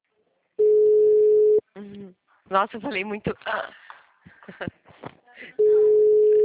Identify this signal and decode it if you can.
Ringback tone (Europe, 425 Hz), with speech.
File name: out- -2013- - - .51